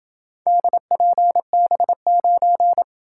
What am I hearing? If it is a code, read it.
Morse code, heard as a keyed tone, about 27 wpm, 700 Hz. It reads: DPB9